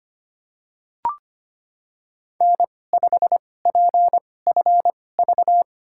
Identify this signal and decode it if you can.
Morse code, heard as a keyed tone, about 25 wpm, spaced as for 18 wpm, 700 Hz. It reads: N5PFV
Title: USA Call Signs - 5 Characters - 2x Character Spacing - 25wpm